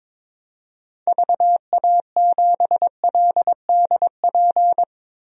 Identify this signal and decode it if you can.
Morse code, heard as a keyed tone, about 21 wpm, 700 Hz. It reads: VA7LDP
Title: DXCC-100 Call Signs - 6 Characters - 22wpm